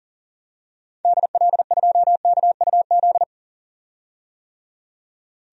Morse code, heard as a keyed tone, about 37 wpm, 700 Hz. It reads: DL2KUZ
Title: DXCC-100 Call Signs - 6 Characters - 40wpm